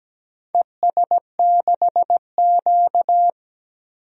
Morse code, heard as a keyed tone, about 17 wpm, 700 Hz. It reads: ES6Q
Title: DXCC-100 Call Signs - 4 Characters - 17wpm